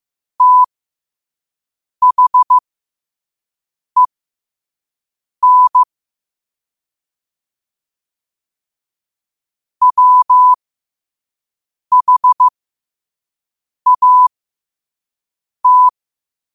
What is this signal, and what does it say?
Morse code, heard as a keyed tone, about 15 wpm, spaced as for 5 wpm, 1000 Hz. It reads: THEN WHAT
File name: qotdfiles / 5wpm